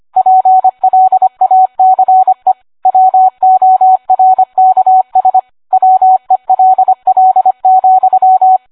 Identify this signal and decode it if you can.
Morse code, heard as a keyed tone, about 25 wpm, 750 Hz, with radio noise. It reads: PLACE WORKS WELL,